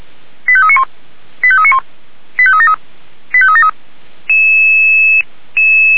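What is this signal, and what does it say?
Five-tone selective call (ZVEI), heard as quick runs of five tones, tones of about 70 ms. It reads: 85391 85391 85373 85373
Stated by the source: ZVEI five-tone selective call received by SDR.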